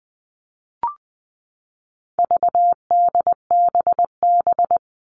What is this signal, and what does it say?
Morse code, heard as a keyed tone, about 20 wpm, 700 Hz. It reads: VDBB
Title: All Cut Numbers - 4-Digit 20wpm